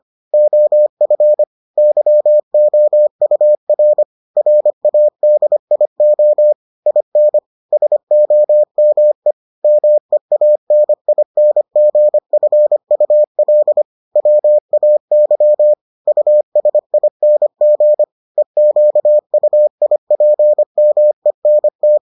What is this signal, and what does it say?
Morse code, heard as a keyed tone, about 25 wpm, 600 Hz. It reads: OF YOUR RADIO IN SOME MEANINGFUL WAY USING EQUIPMENT